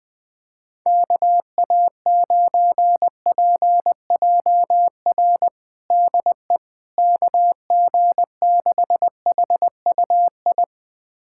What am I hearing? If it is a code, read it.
Morse code, heard as a keyed tone, about 20 wpm, 700 Hz. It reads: KA9PJR DE KG6HUI
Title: USA Callsign DE Callsign 20wpm